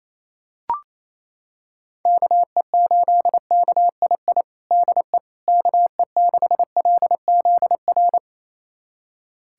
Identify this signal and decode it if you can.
Morse code, heard as a keyed tone, about 28 wpm, 700 Hz. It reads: KE8KII DE KE6LZR
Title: USA Callsign DE Callsign - 6 Characters - 28wpm